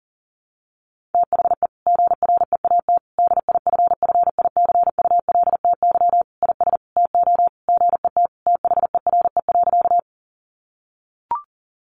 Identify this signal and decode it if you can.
Morse code, heard as a keyed tone, about 40 wpm, 700 Hz. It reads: THE GREAT DIFFICULTY IS TO GET THERE.
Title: Sentences from Top 1500 Words 40wpm